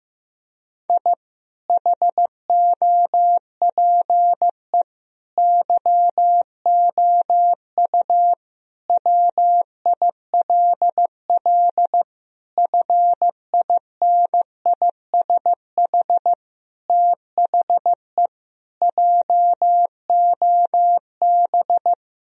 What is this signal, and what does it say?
Morse code, heard as a keyed tone, about 16 wpm, 700 Hz. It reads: I HOPE YOU WILL FINISH THE JOB
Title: Sentences from Top 1000 Words - Review Part 3 - 15wpm